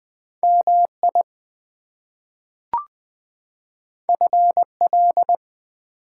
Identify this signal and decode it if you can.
Morse code, heard as a keyed tone, about 20 wpm, 700 Hz. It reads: MI FL
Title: US States and Canadian Provinces 20wpm